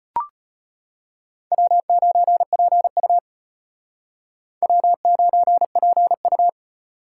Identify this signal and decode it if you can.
Morse code, heard as a keyed tone, about 33 wpm, 700 Hz. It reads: W9PU W9PU